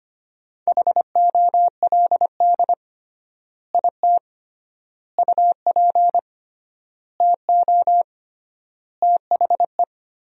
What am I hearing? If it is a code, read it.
Morse code, heard as a keyed tone, about 25 wpm, 700 Hz. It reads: HOLD IT UP TO THE
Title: 3x Word-Spacing - Sentences from Top 400 Words - 25wpm